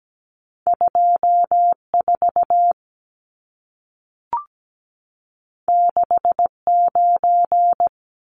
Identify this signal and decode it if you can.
Morse code, heard as a keyed tone, about 17 wpm, 700 Hz. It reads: 24 69
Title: Numbers - 2-Digit 17wpm